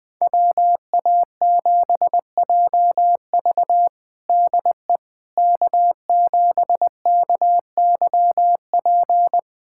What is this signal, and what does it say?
Morse code, heard as a keyed tone, about 20 wpm, 700 Hz. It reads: WA7JV DE K7KYP
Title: USA Callsign DE Callsign - 5 Characters - 20wpm